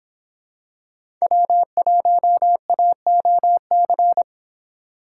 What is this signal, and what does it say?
Morse code, heard as a keyed tone, about 25 wpm, 700 Hz. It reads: W1AOC